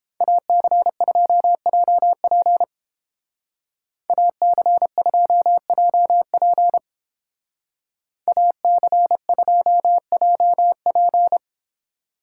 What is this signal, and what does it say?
Morse code, heard as a keyed tone, about 29 wpm, 700 Hz. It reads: AC2JP AC2JP AC2JP